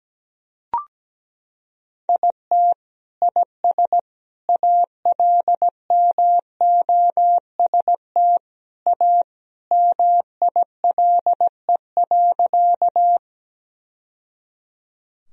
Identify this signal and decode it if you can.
Morse code, heard as a keyed tone, about 17 wpm, 700 Hz. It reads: IT IS ALMOST A MILE.
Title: Sentences from Top 1000 Words - Review Part 4 - 17wpm